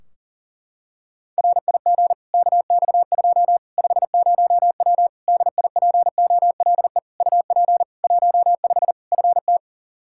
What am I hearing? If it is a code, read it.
Morse code, heard as a keyed tone, about 40 wpm, 700 Hz. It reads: RIG KX2 50W DIPOLE UP 15 FT
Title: K1USN SST QSO Practice 40wpm